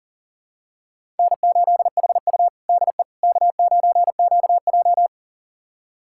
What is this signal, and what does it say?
Morse code, heard as a keyed tone, about 40 wpm, 700 Hz. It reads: N8HU DE K9QJ
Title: USA Callsign DE Callsign - 4 Characters - 40wpm